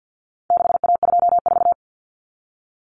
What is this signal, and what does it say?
Morse code, heard as a keyed tone, about 50 wpm, 700 Hz. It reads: 6A24
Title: Common Cut Numbers - 4-Digit 50wpm